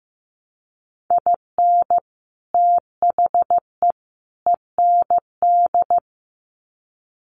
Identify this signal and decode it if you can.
Morse code, heard as a keyed tone, about 15 wpm, 700 Hz. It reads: IN THE END
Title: Sets of 3 Words 15wpm